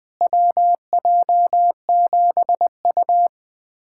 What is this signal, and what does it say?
Morse code, heard as a keyed tone, about 21 wpm, 700 Hz. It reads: WJ7U
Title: USA Callsign DE Callsign - 4 Characters - 20wpm